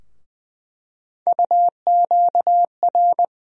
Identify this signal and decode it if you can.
Morse code, heard as a keyed tone, about 20 wpm, 700 Hz. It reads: UQR
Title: USA Call Sign Postfixes - 3 Letters 20wpm